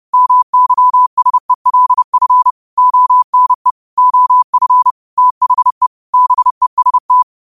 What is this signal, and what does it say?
Morse code, heard as a keyed tone, about 30 wpm, 1000 Hz. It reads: MYSELF ONE OF THE BEST
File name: qotdfiles / 30wpm